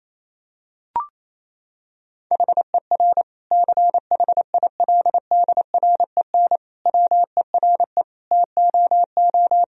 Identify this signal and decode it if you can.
Morse code, heard as a keyed tone, about 28 wpm, 700 Hz. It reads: HER CHILDREN WERE TOO
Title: Sentences from Top 1000 Words - Review Part 4 - 28wpm